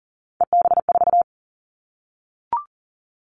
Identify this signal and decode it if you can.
Morse code, heard as a keyed tone, about 40 wpm, 700 Hz. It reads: EB4
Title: Callsign Prefixes for Top 25 DXCC - Repeated 3x - 40wpm